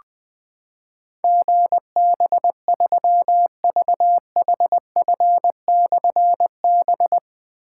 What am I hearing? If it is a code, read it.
Morse code, heard as a keyed tone, about 21 wpm, 700 Hz. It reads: GB3VHF/B
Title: Difficult Call Signs 20wpm